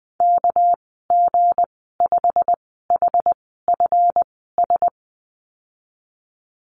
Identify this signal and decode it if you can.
Morse code, heard as a keyed tone, about 20 wpm, spaced as for 15 wpm, 700 Hz. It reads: KG5HFS
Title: 213 Lesson 34g - Number 8 - Call Signs 20wpm